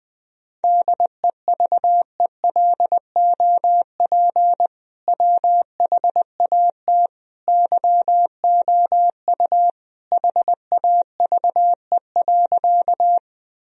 Morse code, heard as a keyed tone, about 20 wpm, 700 Hz. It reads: DEVELOP WHAT YOU HAVE.
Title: Sentences from Top 1000 Words - Review Part 2 - 20wpm